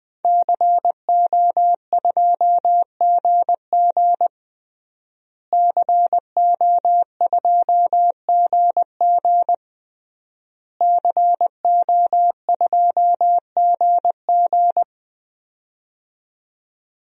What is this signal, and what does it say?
Morse code, heard as a keyed tone, about 20 wpm, 700 Hz. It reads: CO2GG CO2GG CO2GG
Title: DXCC-100 Call Signs - 5 Characters - Repeated 3x - 20wpm